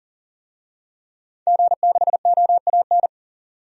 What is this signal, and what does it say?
Morse code, heard as a keyed tone, about 40 wpm, 700 Hz. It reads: G6OAN
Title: DXCC-100 Call Signs - 5 Characters - 40wpm